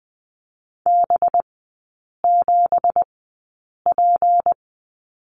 Morse code, heard as a keyed tone, about 20 wpm, spaced as for 8 wpm, 690 Hz. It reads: B7P
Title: 212 Lesson 34e - Number 8 - Three Characters 20wpm